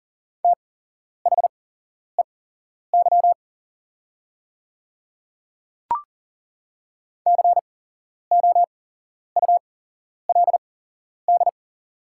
Morse code, heard as a keyed tone, about 40 wpm, spaced as for 11 wpm, 700 Hz. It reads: THEY COULD
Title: 8x Character Spacing - Top 100 Words - 40wpm